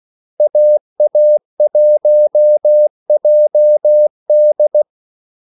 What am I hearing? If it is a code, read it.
Morse code, heard as a keyed tone, about 16 wpm, 600 Hz. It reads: AA1JD
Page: Kaldesignaler 16 wpm